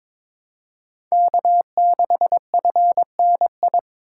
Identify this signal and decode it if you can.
Morse code, heard as a keyed tone, about 22 wpm, 690 Hz. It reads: K6FNI
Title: USA Call Signs - 5 Characters - 2x Character Spacing - 22wpm